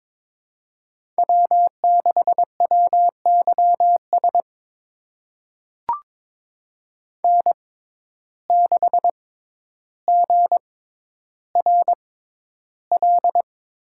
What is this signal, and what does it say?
Morse code, heard as a keyed tone, about 22 wpm, spaced as for 11 wpm, 700 Hz. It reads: W6WYS N6GRL
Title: USA Call Signs - 5 Characters - 6x Character Spacing - 22wpm